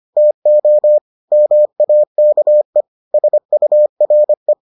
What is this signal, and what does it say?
Morse code, heard as a keyed tone, about 25 wpm, 600 Hz. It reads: TO MAKE SURE